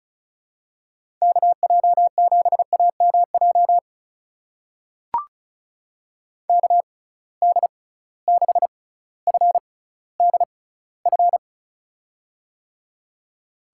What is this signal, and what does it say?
Morse code, heard as a keyed tone, about 36 wpm, spaced as for 18 wpm, 700 Hz. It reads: KJ7AMJ KD6FDF
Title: USA Call Signs - 6 Characters - 6x Character Spacing - 35wpm